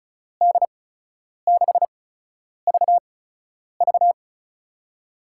Morse code, heard as a keyed tone, about 36 wpm, spaced as for 10 wpm, 700 Hz. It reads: D6VV
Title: USA Call Signs - 8x Character Spacing - 35wpm